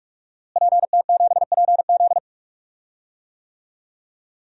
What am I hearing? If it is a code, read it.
Morse code, heard as a keyed tone, about 46 wpm, 700 Hz. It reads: PT7PZ
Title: DXCC-100 Call Signs - 5 Characters - Repeated 2x - 45wpm